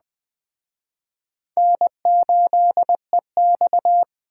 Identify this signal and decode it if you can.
Morse code, heard as a keyed tone, about 20 wpm, 700 Hz. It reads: N8EX